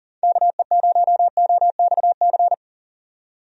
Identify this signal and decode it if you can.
Morse code, heard as a keyed tone, about 40 wpm, 700 Hz. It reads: KE0OXC
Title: USA Call Signs - 6 Characters - 40wpm